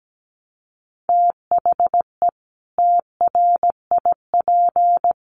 Morse code, heard as a keyed tone, about 17 wpm, 700 Hz. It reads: THE TRIP
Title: Sentences from Top 1300 Words 17wpm